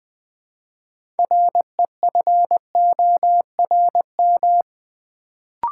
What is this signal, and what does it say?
Morse code, heard as a keyed tone, about 20 wpm, 700 Hz. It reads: REFORM
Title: Build A Word - 3 Letters At A Time - 6-Letter Words - 20wpm